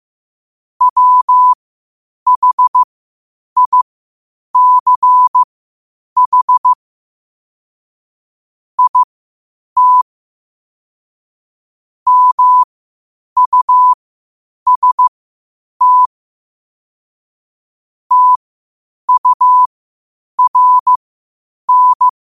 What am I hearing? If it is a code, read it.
Morse code, heard as a keyed tone, about 15 wpm, spaced as for 8 wpm, 1000 Hz. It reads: WHICH IT MUST TURN
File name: qotdfiles / 8wpm